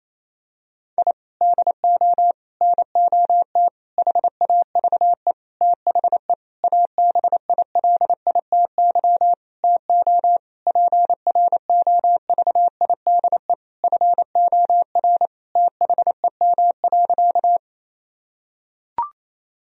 Morse code, heard as a keyed tone, about 28 wpm, 700 Hz. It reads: I DO NOT HAVE THE ABILITY TO PROVIDE FOR THEM.
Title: Sentences from Top 1000 Words - Review Part 3 - 28wpm